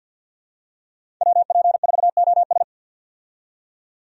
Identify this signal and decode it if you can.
Morse code, heard as a keyed tone, about 50 wpm, 700 Hz. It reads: WP4OS
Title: USA Call Signs - 5 Characters - 50wpm